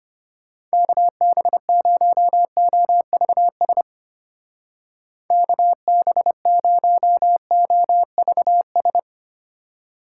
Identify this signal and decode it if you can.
Morse code, heard as a keyed tone, about 32 wpm, 700 Hz. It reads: KB0OVS KB0OVS